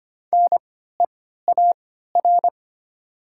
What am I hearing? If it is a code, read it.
Morse code, heard as a keyed tone, about 25 wpm, spaced as for 14 wpm, 700 Hz. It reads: NEAR
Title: 3x Character Spacing - Top 200 Words - 25wpm